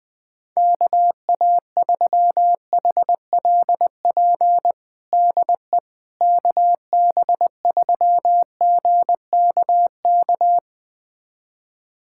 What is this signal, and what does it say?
Morse code, heard as a keyed tone, about 20 wpm, 700 Hz. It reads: KA3HLP DE KB3GKK
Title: USA Callsign DE Callsign 20wpm